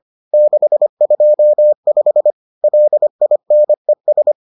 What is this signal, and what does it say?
Morse code, heard as a keyed tone, about 25 wpm, 600 Hz. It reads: 625 LINES